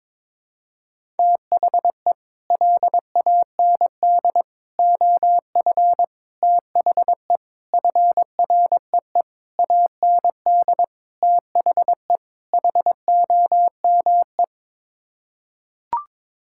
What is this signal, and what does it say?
Morse code, heard as a keyed tone, about 22 wpm, 700 Hz. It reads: THE LAND OF THE FREE AND THE HOME
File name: practice / 22wpm / Sets of 8 Words - 22wpm